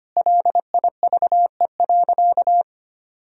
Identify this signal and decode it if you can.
Morse code, heard as a keyed tone, about 25 wpm, 700 Hz. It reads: LIVE.
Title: Sentences from Top 1000 Words - Review Part 3 - 25wpm